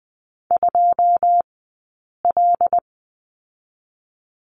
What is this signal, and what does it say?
Morse code, heard as a keyed tone, about 20 wpm, 700 Hz. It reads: 2 L
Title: 092 Lesson 16d - Letter c - Two Characters 20wpm